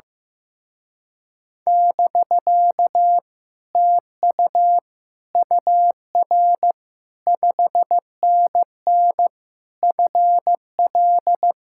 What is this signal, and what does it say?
Morse code, heard as a keyed tone, about 15 wpm, 700 Hz. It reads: <BK> TU UR 5NN FL
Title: Speed-Racing Common QSO Phrases Encore - 15wpm